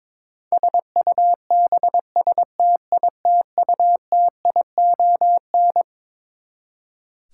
Morse code, heard as a keyed tone, about 23 wpm, 700 Hz. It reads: SUBSTITUTION
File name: Twelve Letter Words - 22wpm